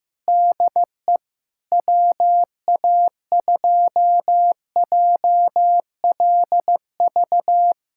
Morse code, heard as a keyed tone, about 15 wpm, 700 Hz. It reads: DE WA2JLV
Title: USA Callsign DE Callsign - 6 Characters - 15wpm